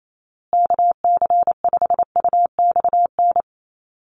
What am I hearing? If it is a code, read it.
Morse code, heard as a keyed tone, about 28 wpm, 700 Hz. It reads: KC5UXN
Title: USA Call Signs Repeated 2x 28wpm